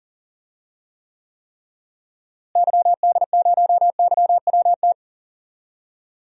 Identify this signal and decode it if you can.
Morse code, heard as a keyed tone, about 38 wpm, 700 Hz. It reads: YD0YWT
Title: DXCC-100 Call Signs - 6 Characters - Repeated 3x - 40wpm